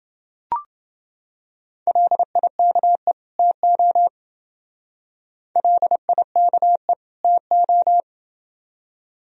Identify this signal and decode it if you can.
Morse code, heard as a keyed tone, about 28 wpm, 700 Hz. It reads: LIKE TO LIKE TO